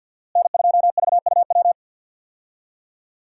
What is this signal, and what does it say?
Morse code, heard as a keyed tone, about 45 wpm, 700 Hz. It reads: N2VUW
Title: USA Call Signs - 5 Characters - Repeated 3x - 50wpm